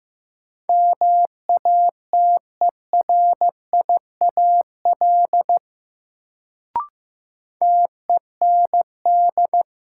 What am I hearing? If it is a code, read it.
Morse code, heard as a keyed tone, about 15 wpm, 700 Hz. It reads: MATERIAL TEND
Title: Sentences from Top 1000 Words - Review Part 5 - New Word Drill - 15wpm